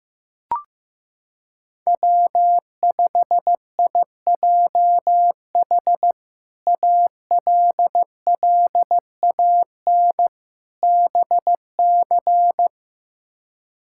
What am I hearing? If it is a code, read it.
Morse code, heard as a keyed tone, about 15 wpm, 700 Hz. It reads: W5IJH ALLAN BC
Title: K1USN SST QSO Practice - 15wpm